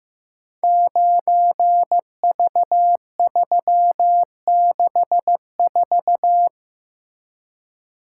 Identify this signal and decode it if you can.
Morse code, heard as a keyed tone, about 16 wpm, 700 Hz. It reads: 9V364
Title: Code Groups - 5 Letter:Numbers - 15wpm